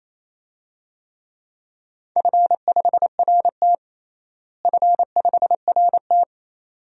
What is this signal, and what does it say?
Morse code, heard as a keyed tone, about 28 wpm, 700 Hz. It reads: F5RT F5RT
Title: DXCC-100 Call Signs - 4 Characters - Repeated 2x - 28wpm